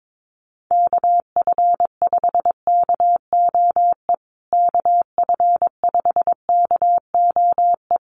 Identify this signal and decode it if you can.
Morse code, heard as a keyed tone, about 22 wpm, 700 Hz. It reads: KF5KOE KF5KOE
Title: USA Call Signs Repeated 3x 22wpm